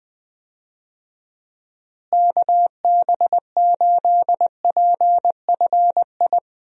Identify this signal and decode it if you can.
Morse code, heard as a keyed tone, about 21 wpm, 700 Hz. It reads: KB8PFI
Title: USA Call Signs - 6 Characters - Repeated 2x - 20wpm